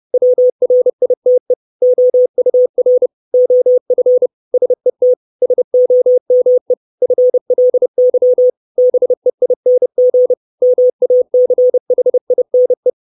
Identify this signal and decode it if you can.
Morse code, heard as a keyed tone, about 30 wpm, 500 Hz. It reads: WRITE OUR OF SET SOME FLY BEING MACHINE
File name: 632 Words 30WPM